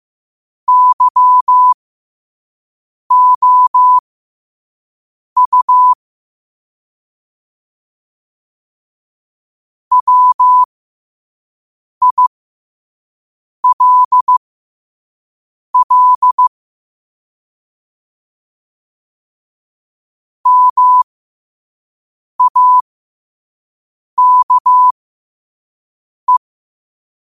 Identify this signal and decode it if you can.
Morse code, heard as a keyed tone, about 15 wpm, spaced as for 5 wpm, 1000 Hz. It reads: YOU WILL MAKE
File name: 5wpm